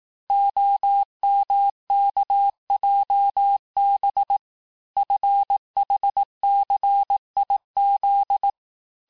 Morse code, heard as a keyed tone, about 18 wpm, 800 Hz. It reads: OMKJB FHCIZ
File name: cours-du-lundi files / cours-1594--teste-chiffre-vitesse-17mpm